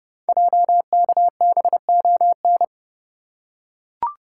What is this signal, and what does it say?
Morse code, heard as a keyed tone, about 30 wpm, 700 Hz. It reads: JKBON
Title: Code Groups - 5 Letter:Numbers - 30wpm